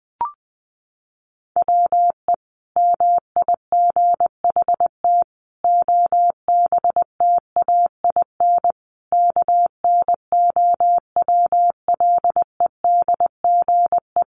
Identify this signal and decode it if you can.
Morse code, heard as a keyed tone, about 20 wpm, 700 Hz. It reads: WE MIGHT OBTAIN KNOWLEDGE
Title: Sentences from Top 1300 Words 20wpm